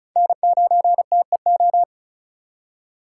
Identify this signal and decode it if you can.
Morse code, heard as a keyed tone, about 36 wpm, 700 Hz. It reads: N9TEO
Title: USA Call Signs - 8x Character Spacing - 35wpm